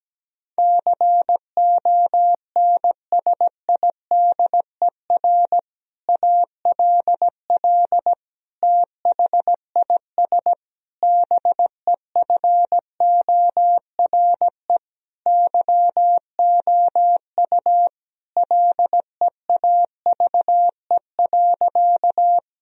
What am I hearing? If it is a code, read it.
Morse code, heard as a keyed tone, about 17 wpm, 700 Hz. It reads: CONSIDER ALL THIS BEFORE YOU LEAVE.
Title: Sentences from Top 1000 Words - Review Part 4 - 17wpm